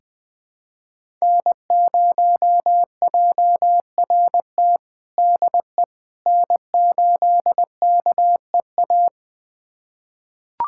Morse code, heard as a keyed tone, about 20 wpm, 700 Hz. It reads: N0JRT DE N8KEA
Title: USA Callsign DE Callsign - 5 Characters - 20wpm